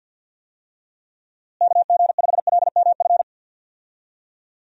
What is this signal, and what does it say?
Morse code, heard as a keyed tone, about 51 wpm, 700 Hz. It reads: KG5LMF
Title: USA Call Signs - 6 Characters - Repeated 2x - 50wpm